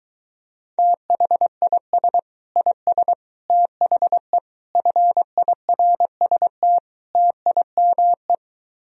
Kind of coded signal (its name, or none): Morse code